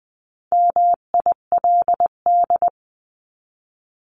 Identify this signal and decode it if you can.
Morse code, heard as a keyed tone, about 20 wpm, 700 Hz. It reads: MILD
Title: 259 Lesson 42d - Words 18-20wpm